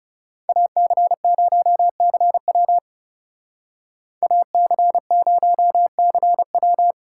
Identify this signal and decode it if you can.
Morse code, heard as a keyed tone, about 39 wpm, 700 Hz. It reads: AC0CW AC0CW